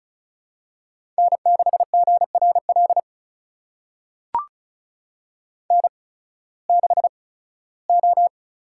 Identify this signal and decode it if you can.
Morse code, heard as a keyed tone, about 36 wpm, spaced as for 19 wpm, 700 Hz. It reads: N6GRL N6O
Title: USA Call Signs - 8x Character Spacing - 35wpm